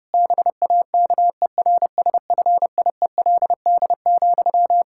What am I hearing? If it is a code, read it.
Morse code, heard as a keyed tone, about 32 wpm, 700 Hz. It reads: BAKERSFIELD,
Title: Practice QSO - Exchange 1 - Beginning - 30wpm